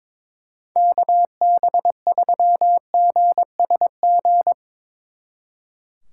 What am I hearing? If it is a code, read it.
Morse code, heard as a keyed tone, about 25 wpm, 700 Hz. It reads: KB3GSG
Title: Speed Racing - USA 2x3 Call Signs 22wpm